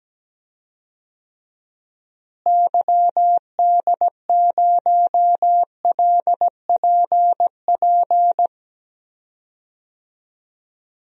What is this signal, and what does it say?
Morse code, heard as a keyed tone, about 17 wpm, 700 Hz. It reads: YD0LPP
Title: DXCC-100 Call Signs - 6 Characters - Repeated 2x - 17wpm